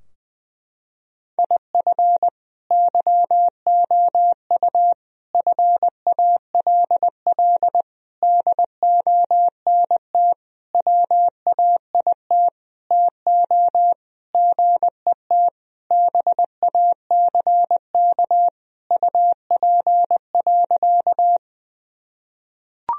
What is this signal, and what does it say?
Morse code, heard as a keyed tone, about 20 wpm, 700 Hz. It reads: IF YOU FALL DONT WAIT TO GET BACK UP.
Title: Sentences from Top 1000 Words - Review Part 2 - 20wpm